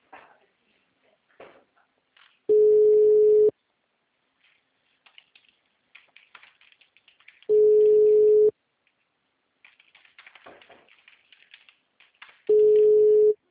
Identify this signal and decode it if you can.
Ringback tone (Europe, 425 Hz).
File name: out- -2024